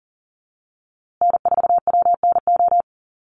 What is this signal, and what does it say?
Morse code, heard as a keyed tone, about 40 wpm, 700 Hz. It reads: N4WNO
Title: USA Call Signs Repeated 2x 40wpm